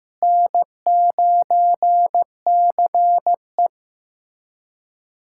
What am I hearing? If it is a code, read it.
Morse code, heard as a keyed tone, about 15 wpm, 700 Hz. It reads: N9CE
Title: USA Callsign DE Callsign - 4 Characters - 15wpm